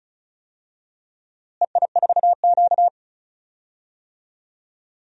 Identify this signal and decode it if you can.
Morse code, heard as a keyed tone, about 34 wpm, 700 Hz. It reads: EI4Q
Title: DXCC-100 Call Signs - 4 Characters - Repeated 3x - 35wpm